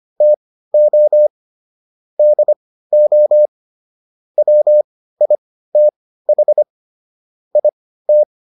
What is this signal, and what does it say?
Morse code, heard as a keyed tone, about 25 wpm, spaced as for 15 wpm, 600 Hz. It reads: TO DO WITH IT